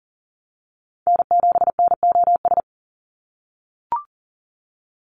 Morse code, heard as a keyed tone, about 39 wpm, 690 Hz. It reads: N7NOS
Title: USA Call Signs 40wpm